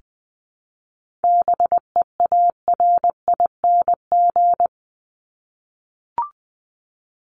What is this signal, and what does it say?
Morse code, heard as a keyed tone, about 20 wpm, 700 Hz. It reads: BEARING
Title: ING Suffix - Top 100 Words with Base Words 20wpm